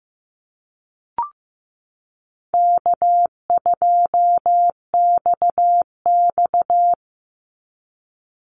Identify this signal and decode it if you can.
Morse code, heard as a keyed tone, about 15 wpm, 700 Hz. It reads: K2XX
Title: Easy Call Signs - 15wpm